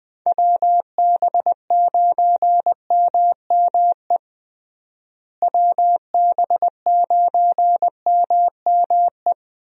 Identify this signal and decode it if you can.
Morse code, heard as a keyed tone, about 21 wpm, 700 Hz. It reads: WB9MME WB9MME
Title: USA Call Signs - 6 Characters - Repeated 3x - 20wpm